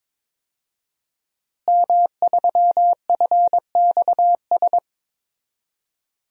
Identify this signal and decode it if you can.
Morse code, heard as a keyed tone, about 22 wpm, 700 Hz. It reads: M3FXS
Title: DXCC-100 Call Signs - 5 Characters - Repeated 2x - 22wpm